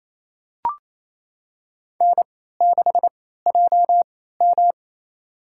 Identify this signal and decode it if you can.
Morse code, heard as a keyed tone, about 28 wpm, spaced as for 16 wpm, 700 Hz. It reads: N6JM
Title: USA Call Signs - 4 Characters - 3x Character Spacing - 28wpm